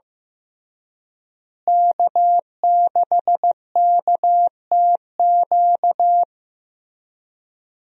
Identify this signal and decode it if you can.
Morse code, heard as a keyed tone, about 15 wpm, 700 Hz. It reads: K6KTQ
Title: USA Call Signs - 5 Characters - Repeated 3x - 15wpm